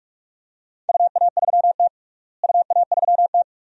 Morse code, heard as a keyed tone, about 44 wpm, 700 Hz. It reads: UA3T UA3T
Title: DXCC-100 Call Signs - 4 Characters - Repeated 2x - 45wpm